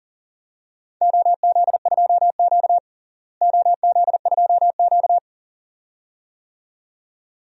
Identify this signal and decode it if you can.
Morse code, heard as a keyed tone, about 40 wpm, 700 Hz. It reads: OZ2Q OZ2Q
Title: DXCC-100 Call Signs - 4 Characters - Repeated 2x - 40wpm